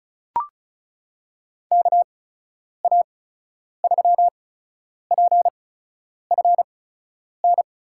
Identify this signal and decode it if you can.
Morse code, heard as a keyed tone, about 36 wpm, spaced as for 10 wpm, 700 Hz. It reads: KA3PFN
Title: USA Call Signs - 8x Character Spacing - 35wpm